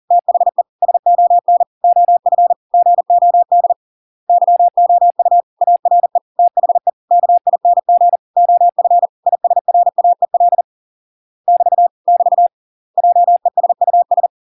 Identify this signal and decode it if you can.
Morse code, heard as a keyed tone, about 40 wpm, 700 Hz. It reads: THE SON OF GOD YOU ARE THE KING OF ISRAEL <BT> <BT> JESUS